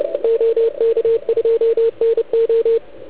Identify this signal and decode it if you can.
Morse code, heard as a keyed tone, about 28 wpm, 460 Hz. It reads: OK2NO